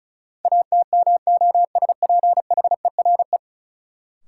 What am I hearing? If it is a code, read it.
Morse code, heard as a keyed tone, about 36 wpm, 700 Hz. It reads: ATMOSPHERE
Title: Sentences from Top 1600 Words - New Word Drill - 35wpm